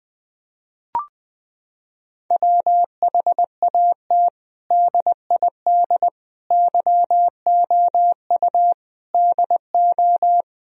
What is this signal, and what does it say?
Morse code, heard as a keyed tone, about 21 wpm, 700 Hz. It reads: WHAT DID YOU DO
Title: Sentences from Top 1000 Words - Review Part 1 - 20wpm